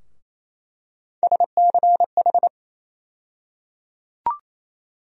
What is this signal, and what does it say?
Morse code, heard as a keyed tone, about 28 wpm, 700 Hz. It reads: SCH
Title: USA Call Sign Postfixes - 3 Letters 28wpm